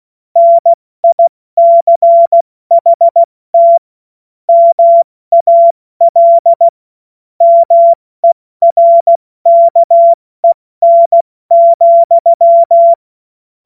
Morse code, heard as a keyed tone, about 16 wpm, 680 Hz. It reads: NICHT MAL MERKEN,
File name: Morsekurs / Witze / Witz 17. Mai 2022 70 F80